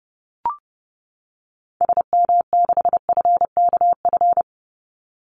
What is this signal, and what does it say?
Morse code, heard as a keyed tone, about 30 wpm, 700 Hz. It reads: SM6FKF
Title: Call Signs - 30wpm